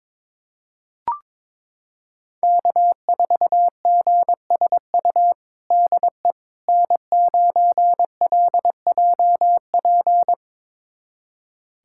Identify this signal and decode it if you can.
Morse code, heard as a keyed tone, about 22 wpm, 700 Hz. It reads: K4GSU DE N9LJP
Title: USA Callsign DE Callsign 22wpm